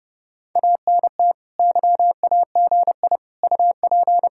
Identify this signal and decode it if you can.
Morse code, heard as a keyed tone, about 37 wpm, 700 Hz. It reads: ANT YAGI UP
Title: Speed-Racing Common QSO Phrases - 30wpm